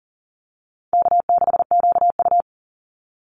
Morse code, heard as a keyed tone, about 40 wpm, 700 Hz. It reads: K6QU
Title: Easy Call Signs Repeated 3x - 40wpm